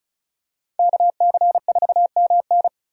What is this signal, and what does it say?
Morse code, heard as a keyed tone, about 36 wpm, 700 Hz. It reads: KC4MN
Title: USA Call Signs - 5 Characters - 6x Character Spacing - 35wpm